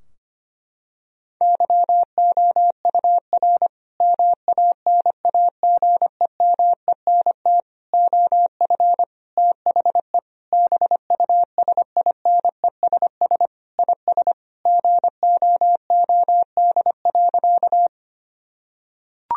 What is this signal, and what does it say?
Morse code, heard as a keyed tone, about 25 wpm, 700 Hz. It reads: YOUR MANAGEMENT OF THE BUSINESS IS GOOD.
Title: Sentences from Top 1000 Words - Review Part 4 - 25wpm